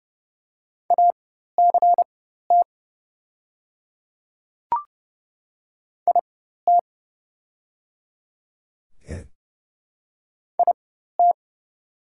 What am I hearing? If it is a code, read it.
Morse code, heard as a keyed tone, about 29 wpm, spaced as for 14 wpm, 700 Hz. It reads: ACT IT IT
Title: 4x Character Spacing - Top 300 Words - 30wpm